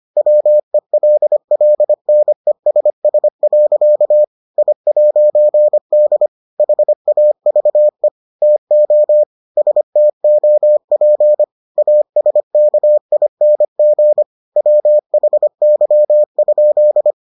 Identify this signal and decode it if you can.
Morse code, heard as a keyed tone, about 25 wpm, 600 Hz. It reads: WELLNESS. I'D HAVE TO STOP ASKING WHY?